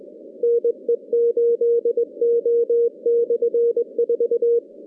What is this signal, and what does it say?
Morse code, heard as a keyed tone, about 19 wpm, 470 Hz. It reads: NE8O/4
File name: contests / CW-SS-k5tr- -2100